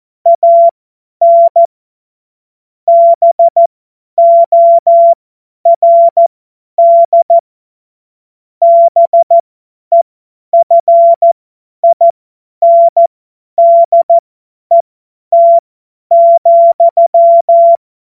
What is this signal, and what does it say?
Morse code, heard as a keyed tone, about 14 wpm, spaced as for 10 wpm, 680 Hz. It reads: AN BORD BEFINDET,